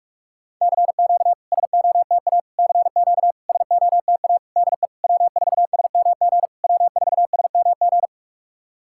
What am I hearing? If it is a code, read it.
Morse code, heard as a keyed tone, about 46 wpm, 690 Hz. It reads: CQ SOTA CQ SOTA DE W4SMG W4SMG
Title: SOTA QSO Practice 45wpm